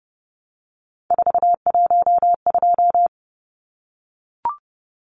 Morse code, heard as a keyed tone, about 30 wpm, 700 Hz. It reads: 412
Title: Numbers - 3-Digit 30wpm